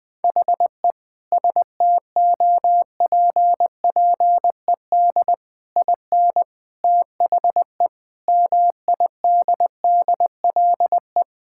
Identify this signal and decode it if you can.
Morse code, heard as a keyed tone, about 20 wpm, 700 Hz. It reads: HE STOPPED IN THE MIDDLE
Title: Sentences with 3-Word Phrases - Spoken Beginning - 20wpm